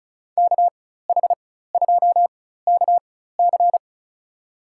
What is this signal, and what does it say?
Morse code, heard as a keyed tone, about 34 wpm, spaced as for 16 wpm, 700 Hz. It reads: KH2KC
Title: USA Call Signs - 5 Characters - 4x Character Spacing - 35wpm